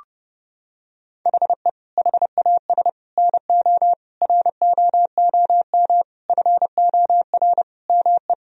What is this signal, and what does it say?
Morse code, heard as a keyed tone, about 29 wpm, 700 Hz. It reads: HE HAS NO ROOM FOR ME
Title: Sentences from Top 1000 Words - Review Part 4 - 30wpm